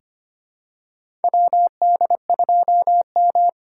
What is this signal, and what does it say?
Morse code, heard as a keyed tone, about 25 wpm, 700 Hz. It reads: WD2M
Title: USA Call Signs - 4 Characters - 3x Character Spacing - 25wpm